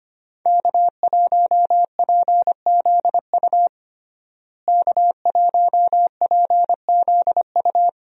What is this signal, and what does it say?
Morse code, heard as a keyed tone, about 25 wpm, 700 Hz. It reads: K1PZU K1PZU
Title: USA Call Signs - 5 Characters - Repeated 2x - 25wpm